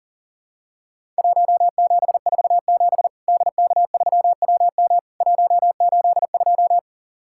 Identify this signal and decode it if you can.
Morse code, heard as a keyed tone, about 39 wpm, 700 Hz. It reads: 1747 DK3WM 182
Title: Worked All Europe DX Contest - QTC Exchange - 40wpm